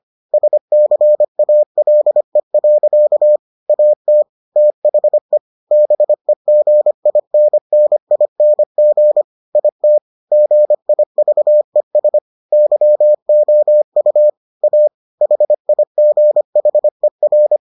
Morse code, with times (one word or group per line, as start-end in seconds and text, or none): 0.3-3.4 SCALE.
3.7-4.2 AT
4.6-5.4 THE
5.7-9.2 BEGINNING
9.6-10.0 IT
10.3-12.2 GIVES
12.5-14.3 YOU
14.6-14.9 A
15.2-17.6 HIGHER